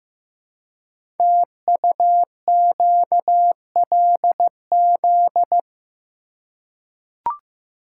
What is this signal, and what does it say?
Morse code, heard as a keyed tone, about 15 wpm, 700 Hz. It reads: TUQLZ
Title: Code Groups - 5 Letters - 15wpm